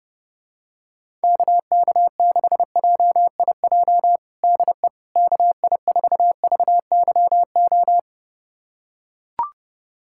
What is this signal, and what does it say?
Morse code, heard as a keyed tone, about 30 wpm, 700 Hz. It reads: KK6JIJ DE KI4VYO
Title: USA Callsign DE Callsign - 6 Characters - 30wpm